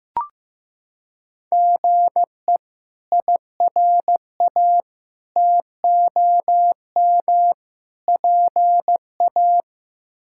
Morse code, heard as a keyed tone, about 15 wpm, 700 Hz. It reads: GE IRA TOM PA
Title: K1USN SST QSO Practice - 15wpm